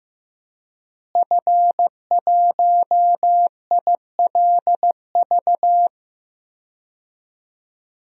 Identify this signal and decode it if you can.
Morse code, heard as a keyed tone, about 15 wpm, 700 Hz. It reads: F1ILV
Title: DXCC-100 Call Signs - 5 Characters - Repeated 3x - 15wpm